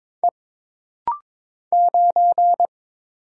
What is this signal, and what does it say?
Morse code, heard as a keyed tone, about 22 wpm, 700 Hz. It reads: E 9
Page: This rapid-fire timing is geared to push your ICR (Instant Character Recognition) skills.